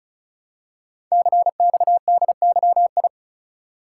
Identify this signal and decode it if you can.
Morse code, heard as a keyed tone, about 36 wpm, 700 Hz. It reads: CXDYI
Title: Code Groups - 5 Letters - 35wpm